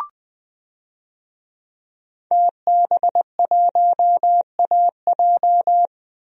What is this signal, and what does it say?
Morse code, heard as a keyed tone, about 20 wpm, 700 Hz. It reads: TB1AJ
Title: DXCC-100 Call Signs - 5 Characters - 20wpm